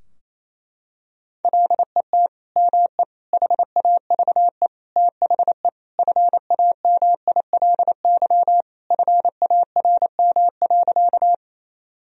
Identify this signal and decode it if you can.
Morse code, heard as a keyed tone, about 27 wpm, 700 Hz. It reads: LET ME HAVE THE FAMILY FARM.
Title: Sentences from Top 1000 Words - Review Part 2 - 28wpm